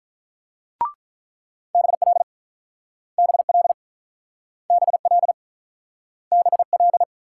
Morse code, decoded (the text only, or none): BL BL BL BL